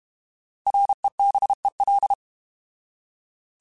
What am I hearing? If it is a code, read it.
Morse code, heard as a keyed tone, about 32 wpm, 800 Hz. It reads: REBEL
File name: Alphabetical / Alphabetical mp3s / Rs @30